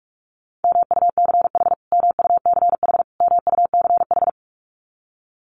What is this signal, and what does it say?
Morse code, heard as a keyed tone, about 44 wpm, 700 Hz. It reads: MUCH MUCH MUCH
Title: Top 200 Words Repeated 45wpm